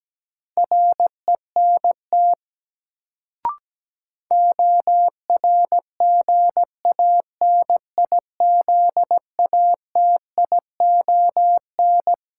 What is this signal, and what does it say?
Morse code, heard as a keyed tone, about 17 wpm, 700 Hz. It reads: RENT ORGANIZATION
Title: Sentences from Top 1000 Words - Review Part 5 - New Word Drill - 17wpm